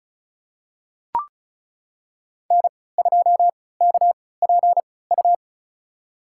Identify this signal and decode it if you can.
Morse code, heard as a keyed tone, about 36 wpm, spaced as for 20 wpm, 700 Hz. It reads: N2KPU
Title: USA Call Signs - 5 Characters - 3x Character Spacing - 35wpm